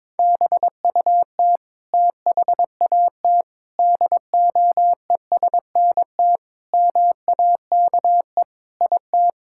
Morse code, heard as a keyed tone, about 22 wpm, 700 Hz. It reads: BUT THAT DOESNT MAKE IT
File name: practice / 22wpm / Sets of 7 Words - 22wpm